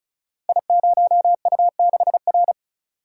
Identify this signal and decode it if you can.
Morse code, heard as a keyed tone, about 36 wpm, 690 Hz. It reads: I0U6R
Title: Code Groups - 5 Letter:Numbers - 35wpm